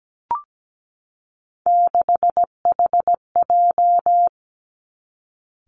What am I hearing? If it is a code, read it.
Morse code, heard as a keyed tone, about 17 wpm, 700 Hz. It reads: 6HJ
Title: Three Challenging Letter-Number Combinations 17wpm